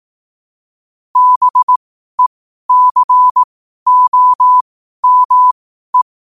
Morse code, heard as a keyed tone, about 18 wpm, spaced as for 12 wpm, 1000 Hz. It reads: BECOME
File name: qotdfiles / 12wpm